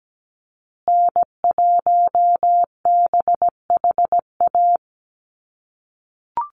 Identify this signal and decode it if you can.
Morse code, heard as a keyed tone, about 17 wpm, 700 Hz. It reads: N1BHA
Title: USA Call Signs Repeated 3x 17wpm